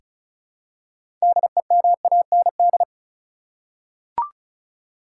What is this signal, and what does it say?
Morse code, heard as a keyed tone, about 36 wpm, 700 Hz. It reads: DEMAND
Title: Six Letter Words - 35wpm